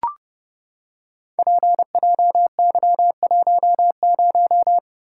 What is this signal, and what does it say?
Morse code, heard as a keyed tone, about 30 wpm, 700 Hz. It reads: PJY10
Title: Code Groups - 5 Letter:Numbers - 30wpm